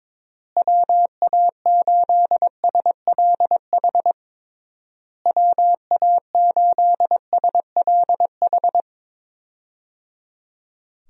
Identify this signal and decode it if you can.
Morse code, heard as a keyed tone, about 22 wpm, 700 Hz. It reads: WA8SLH WA8SLH
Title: USA Call Signs - 6 Characters - Repeated 3x - 22wpm